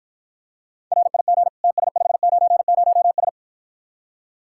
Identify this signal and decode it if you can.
Morse code, heard as a keyed tone, about 53 wpm, 700 Hz. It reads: RIG TS590S